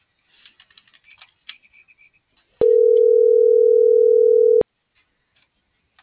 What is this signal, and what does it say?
Ringback tone (North America), 440 + 480 Hz.